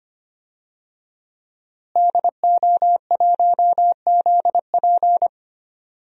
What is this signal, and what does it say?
Morse code, heard as a keyed tone, about 25 wpm, 700 Hz. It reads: DO1ZP
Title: DXCC-100 Call Signs - 5 Characters - Repeated 3x - 25wpm